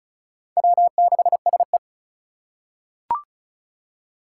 Morse code, heard as a keyed tone, about 34 wpm, 700 Hz. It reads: W6SE
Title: USA Call Signs - 4 Characters - 8x Character Spacing - 35wpm